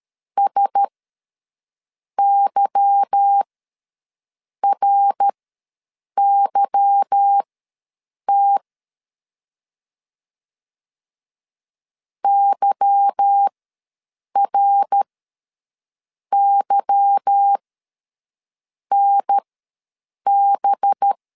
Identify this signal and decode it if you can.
Morse code, heard as a keyed tone, about 13 wpm, spaced as for 6 wpm, 800 Hz. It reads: SYRYT YRYNB
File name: LEZIONI / 8 YJ45 / 8 Y ws 12wpm 3409